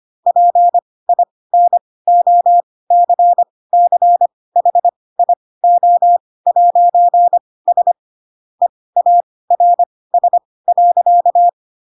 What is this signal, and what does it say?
Morse code, heard as a keyed tone, about 25 wpm, spaced as for 18 wpm, 700 Hz. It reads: PINOCCHIO'S EARS.
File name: Pinocchio ch3 12 25 ik1hge